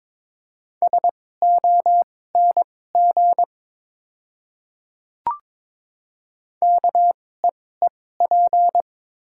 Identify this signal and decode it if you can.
Morse code, heard as a keyed tone, about 22 wpm, spaced as for 16 wpm, 700 Hz. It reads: SONG KEEP
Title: 2x Character Spacing - Top 500 Words - 22wpm